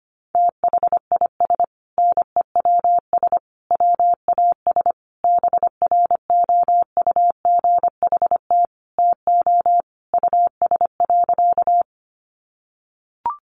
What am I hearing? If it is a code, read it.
Morse code, heard as a keyed tone, about 25 wpm, 700 Hz. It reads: THIS NEWS WAS BROUGHT TO US.
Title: Sentences from Top 800 Words 25wpm